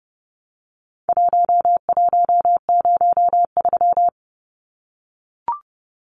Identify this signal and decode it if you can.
Morse code, heard as a keyed tone, about 30 wpm, 700 Hz. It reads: 1103
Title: Numbers - Contest Style - 4-Digit 30wpm